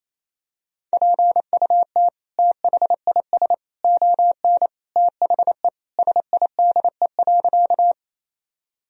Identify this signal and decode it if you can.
Morse code, heard as a keyed tone, about 28 wpm, 700 Hz. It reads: PUT THIS ON THE SIDE.
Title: Sentences from Top 1000 Words - Review Part 1 - 28wpm